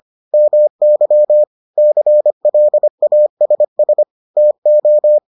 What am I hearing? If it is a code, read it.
Morse code, heard as a keyed tone, about 25 wpm, 600 Hz. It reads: MY CLASS TO